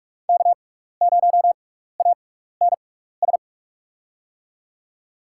Morse code, heard as a keyed tone, about 45 wpm, spaced as for 15 wpm, 700 Hz. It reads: K0ANS
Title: USA Call Signs - 5 Characters - 6x Character Spacing - 45wpm